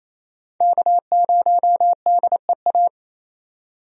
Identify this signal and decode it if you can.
Morse code, heard as a keyed tone, about 28 wpm, 700 Hz. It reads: K0DEA
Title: USA Call Signs - 5 Characters - 28wpm